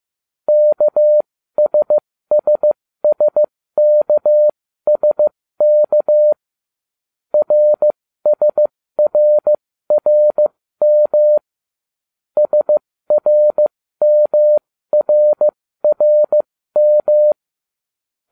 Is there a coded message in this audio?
Yes: Morse code from 0.5 to 17.3 s, heard as a keyed tone, about 15 wpm, spaced as for 13 wpm, 600 Hz. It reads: KSSSKSK RSRRM SRMRRM